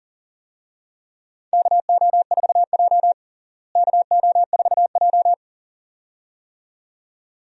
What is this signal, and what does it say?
Morse code, heard as a keyed tone, about 40 wpm, 700 Hz. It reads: KO4J KO4J
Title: USA Call Signs - 4 Characters - Repeated 2x - 40wpm